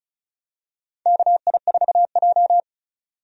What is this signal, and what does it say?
Morse code, heard as a keyed tone, about 36 wpm, 700 Hz. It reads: KI4J
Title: USA Call Signs - 4 Characters - 35wpm